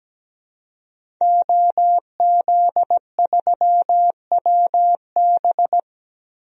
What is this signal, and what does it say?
Morse code, heard as a keyed tone, about 17 wpm, 700 Hz. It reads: OZ3WB
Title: DXCC-100 Call Signs - 5 Characters - Repeated 3x - 17wpm